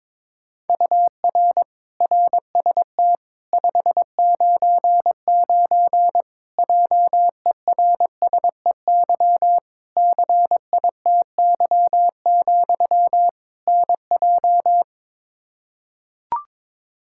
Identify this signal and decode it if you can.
Morse code, heard as a keyed tone, about 22 wpm, 700 Hz. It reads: UR RST 599 JERSEY CITY, NJ
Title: Practice QSO - Exchange 1 - Beginning - 22wpm